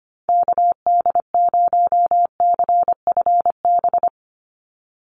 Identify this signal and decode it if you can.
Morse code, heard as a keyed tone, about 25 wpm, 700 Hz. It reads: KD0CFB
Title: USA Call Signs 25wpm